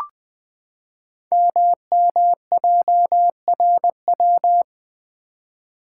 Morse code, heard as a keyed tone, about 20 wpm, 700 Hz. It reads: MMJRW
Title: Code Groups - 5 Letters - 20wpm